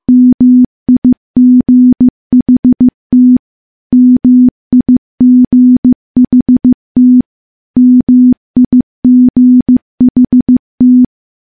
Morse code, heard as a keyed tone, about 15 wpm, 250 Hz. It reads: MIGHT MIGHT MIGHT
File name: morse / morse-training-15WPM